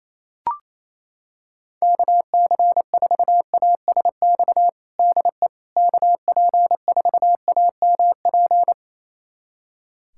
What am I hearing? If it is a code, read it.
Morse code, heard as a keyed tone, about 28 wpm, 700 Hz. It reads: KC4ASX DE KP4AMP
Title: USA Callsign DE Callsign 28wpm